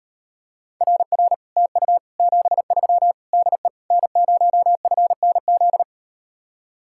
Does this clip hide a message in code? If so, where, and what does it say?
Yes: Morse code from 0.8 to 5.8 s, heard as a keyed tone, about 38 wpm, 700 Hz. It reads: RR TU 73 DE N0FNZ